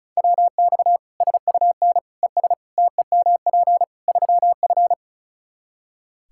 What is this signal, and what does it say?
Morse code, heard as a keyed tone, about 36 wpm, 700 Hz. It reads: WX SUN ES TEMP 3F
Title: Practice QSO - Exchange 2 - Info - 35wpm